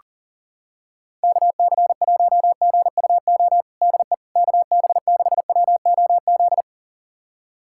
Morse code, heard as a keyed tone, about 40 wpm, 700 Hz. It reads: KC1GUO DE KB6WOZ
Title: USA Callsign DE Callsign 40wpm